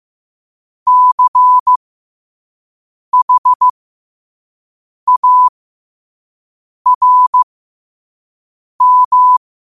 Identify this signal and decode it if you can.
Morse code, heard as a keyed tone, about 15 wpm, spaced as for 5 wpm, 1000 Hz. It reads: CHARM
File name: qotdfiles / 5wpm